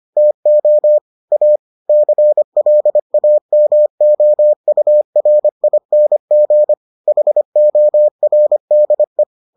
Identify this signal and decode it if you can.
Morse code, heard as a keyed tone, about 25 wpm, 600 Hz. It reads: TO A CLAMOURING HORDE